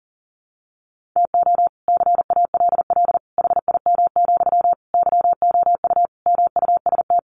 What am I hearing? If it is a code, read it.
Morse code, heard as a keyed tone, about 40 wpm, 700 Hz. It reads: TO CALL HIM, YOU MUST
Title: Sentences from Top 1400 Words 40wpm